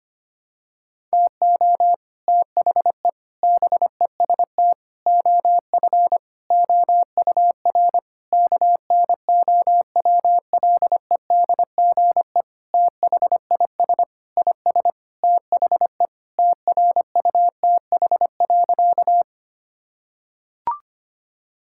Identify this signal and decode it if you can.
Morse code, heard as a keyed tone, about 25 wpm, 700 Hz. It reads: TO THE BEST OF OUR KNOWLEDGE THIS IS THE TRUTH.
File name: practice / 25wpm / Sentences with 9-Word Phrases - 25wpm